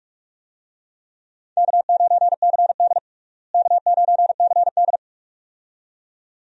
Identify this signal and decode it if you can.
Morse code, heard as a keyed tone, about 42 wpm, 700 Hz. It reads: K9CD K9CD
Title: USA Call Signs - 4 Characters - Repeated 2x - 45wpm